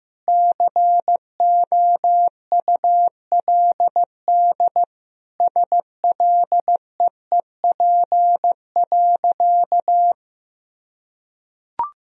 Morse code, heard as a keyed tone, about 16 wpm, 700 Hz. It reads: COULD SLEEP.
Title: Sentences from Top 500 Words and Abbreviations 15wpm